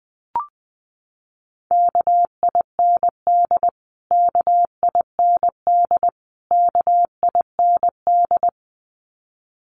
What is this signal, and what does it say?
Morse code, heard as a keyed tone, about 20 wpm, 700 Hz. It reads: KIND KIND KIND
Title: Top 300 Words Repeated 20wpm